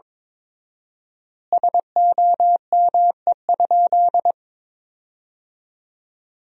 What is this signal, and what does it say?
Morse code, heard as a keyed tone, about 23 wpm, 700 Hz. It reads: SOME?
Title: 4x Word-Spacing - Sentences from Top 100 Words - 22wpm